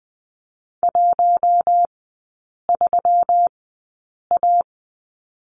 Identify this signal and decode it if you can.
Morse code, heard as a keyed tone, about 20 wpm, spaced as for 8 wpm, 700 Hz. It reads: 13A
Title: 191 Lesson 31e - Letter v - Three Characters 20wpm